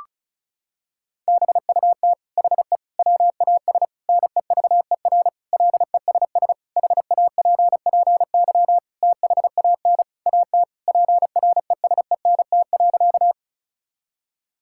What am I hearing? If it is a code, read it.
Morse code, heard as a keyed tone, about 36 wpm, 700 Hz. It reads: BUT HE WAS NEVER LESS HAPPY THAN AT PRESENT.
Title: Sentences from Top 1000 Words - Review Part 5 - 35wpm